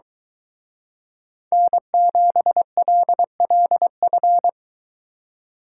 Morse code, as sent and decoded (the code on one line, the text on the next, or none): -. --... .-.. .-.. ..-.
N7LLF